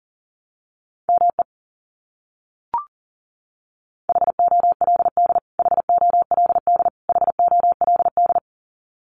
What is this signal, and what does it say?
Morse code, heard as a keyed tone, about 40 wpm, 700 Hz. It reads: ME HOLD HOLD HOLD
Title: Top 500 Words Repeated 40wpm